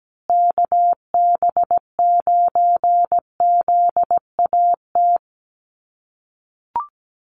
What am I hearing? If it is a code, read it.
Morse code, heard as a keyed tone, about 17 wpm, 700 Hz. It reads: KB9ZAT
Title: USA Call Signs Repeated 3x 17wpm